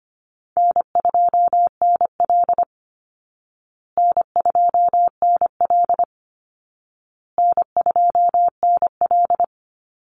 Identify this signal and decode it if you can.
Morse code, heard as a keyed tone, about 25 wpm, 700 Hz. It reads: N2NL N2NL N2NL
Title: Easy Call Signs Repeated 3x - 25wpm